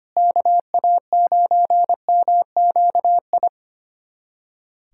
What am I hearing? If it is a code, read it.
Morse code, heard as a keyed tone, about 25 wpm, 700 Hz. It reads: KA9MQI
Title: USA Callsign DE Callsign - 6 Characters - 25wpm